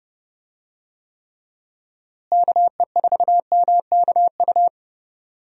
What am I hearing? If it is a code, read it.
Morse code, heard as a keyed tone, about 30 wpm, 700 Hz. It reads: KE4MKU
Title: USA Call Signs - 6 Characters - Repeated 3x - 30wpm